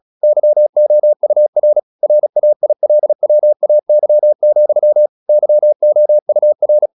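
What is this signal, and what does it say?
Morse code, heard as a keyed tone, about 36 wpm, 600 Hz. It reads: YOUR RAILWAY, YOUR
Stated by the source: Morse practice at 36 wpm, text of Ken Follett's novel Never, from CW med Gnister.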